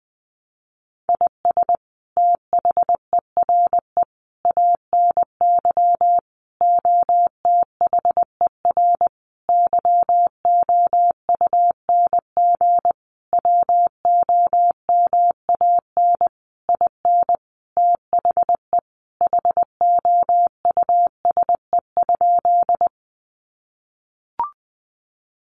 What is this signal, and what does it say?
Morse code, heard as a keyed tone, about 20 wpm, 700 Hz. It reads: IS THERE ANY OTHER YOUNG WOMAN IN THE HOUSE?
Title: Sentences from Top 1100 Words 20wpm